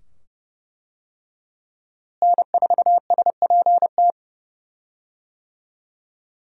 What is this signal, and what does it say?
Morse code, heard as a keyed tone, about 30 wpm, 700 Hz. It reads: N4SPT
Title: USA Call Signs - 5 Characters - Repeated 2x - 30wpm